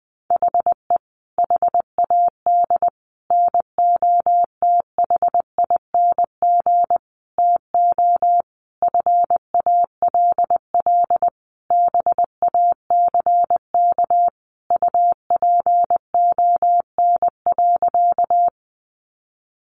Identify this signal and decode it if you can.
Morse code, heard as a keyed tone, about 20 wpm, 700 Hz. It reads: HE HAD NOTHING TO FALL BACK UPON.
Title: Sentences from Top 1100 Words 20wpm